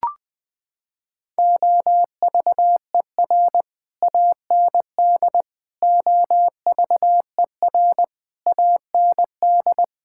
Morse code, heard as a keyed tone, about 21 wpm, 700 Hz. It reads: OVER AND OVER AND
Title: Sets of 6 Words - 20wpm